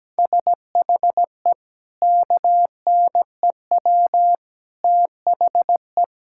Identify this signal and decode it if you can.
Morse code, heard as a keyed tone, about 17 wpm, 700 Hz. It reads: SHE KNEW THE
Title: Sentences from Top 1000 Words - Review Part 6 - 17wpm